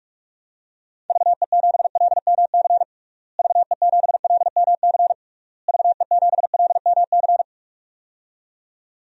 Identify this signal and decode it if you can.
Morse code, heard as a keyed tone, about 46 wpm, 700 Hz. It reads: VE7LMC VE7LMC VE7LMC
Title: DXCC-100 Call Signs - 6 Characters - Repeated 3x - 45wpm